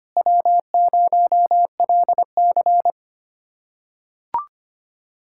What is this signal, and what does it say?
Morse code, heard as a keyed tone, about 25 wpm, 700 Hz. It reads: W0LC
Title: USA Callsign DE Callsign - 4 Characters - 25wpm